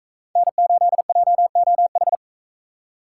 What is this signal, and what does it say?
Morse code, heard as a keyed tone, about 41 wpm, 700 Hz. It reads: N8JOH